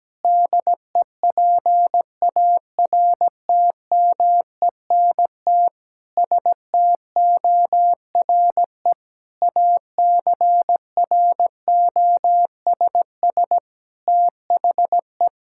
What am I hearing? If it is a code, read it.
Morse code, heard as a keyed tone, about 17 wpm, 700 Hz. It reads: DEPARTMENT STORE ACROSS THE
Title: Sentences from Top 1000 Words - Review Part 5 - 17wpm